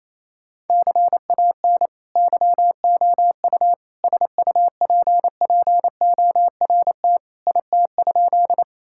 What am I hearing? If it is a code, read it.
Morse code, heard as a keyed tone, about 28 wpm, 700 Hz. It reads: CAN YOU SUPPORT IT?
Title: Sentences from Top 1000 Words - Review Part 5 - 28wpm